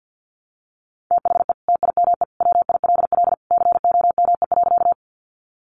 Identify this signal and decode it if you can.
Morse code, heard as a keyed tone, about 50 wpm, 700 Hz. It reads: THE TIME WILL COME.
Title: Sentences from Top 100 Words 50wpm